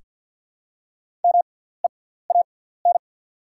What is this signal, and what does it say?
Morse code, heard as a keyed tone, about 48 wpm, spaced as for 17 wpm, 690 Hz. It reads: MEAN
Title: 6x Character Spacing - Top 200 Words - 50wpm